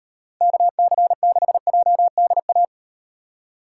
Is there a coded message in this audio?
Yes: Morse code from 0.4 to 2.6 s, heard as a keyed tone, about 38 wpm, 700 Hz. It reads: KC6JDA